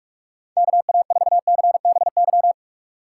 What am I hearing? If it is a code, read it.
Morse code, heard as a keyed tone, about 46 wpm, 700 Hz. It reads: KA4CBY
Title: USA Call Signs - 6 Characters - 4x Character Spacing - 45wpm